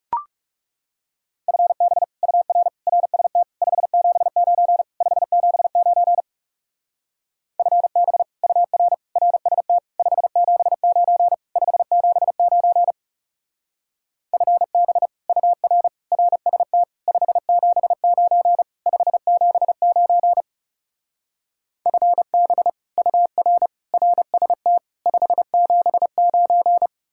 Morse code, heard as a keyed tone, about 36 wpm, 700 Hz. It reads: FB UR RST 579 579 FB UR RST 579 579 FB UR RST 579 579 FB UR RST 579
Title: Speed-Racing Common QSO Phrases Encore - 30wpm